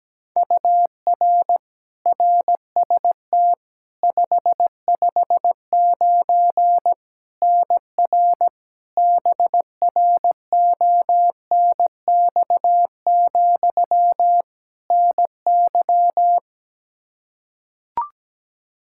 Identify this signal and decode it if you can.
Morse code, heard as a keyed tone, about 17 wpm, 700 Hz. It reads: UR RST 559 NR BRONX, NY
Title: K1USN SST QSO Practice 17wpm